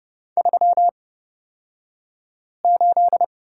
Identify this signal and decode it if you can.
Morse code, heard as a keyed tone, about 30 wpm, spaced as for 5 wpm, 700 Hz. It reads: 38
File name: Lesson 38d - Letter q - Two Characters 30wpm